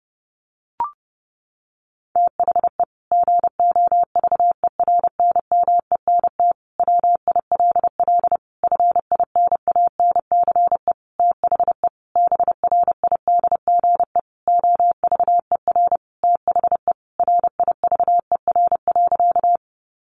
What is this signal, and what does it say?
Morse code, heard as a keyed tone, about 30 wpm, 700 Hz. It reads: THE GOVERNMENT WILL FINANCE THE BRIDGE OVER THE RIVER.
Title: Sentences from Top 1500 Words 30wpm